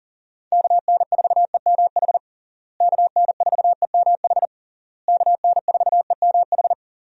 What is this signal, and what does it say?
Morse code, heard as a keyed tone, about 40 wpm, 700 Hz. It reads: KN4EMH KN4EMH KN4EMH
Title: USA Call Signs - 6 Characters - Repeated 3x - 40wpm